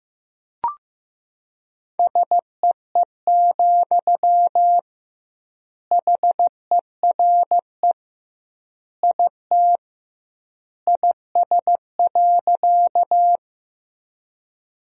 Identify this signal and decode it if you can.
Morse code, heard as a keyed tone, about 15 wpm, 700 Hz. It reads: SEE, HERE IT IS.
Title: 2x Word-Spacing - Sentences from Top 200 Words - 15wpm